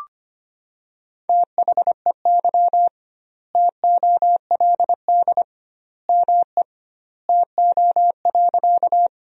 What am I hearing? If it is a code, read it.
Morse code, heard as a keyed tone, about 25 wpm, 700 Hz. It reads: THEY TOLD ME TO.
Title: 2x Word-Spacing - Sentences from Top 400 Words - 25wpm